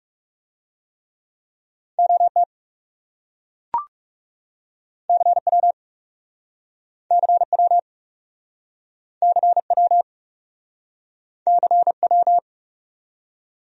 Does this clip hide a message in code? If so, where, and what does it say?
Yes: Morse code from 2.0 to 12.4 s, heard as a keyed tone, about 42 wpm, 700 Hz. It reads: OT CW CW CW CW